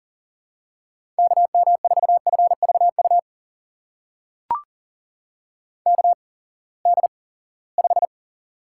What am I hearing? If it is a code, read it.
Morse code, heard as a keyed tone, about 40 wpm, spaced as for 23 wpm, 700 Hz. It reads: KM4FVU KD5
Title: USA Call Signs - 6 Characters - 8x Character Spacing - 40wpm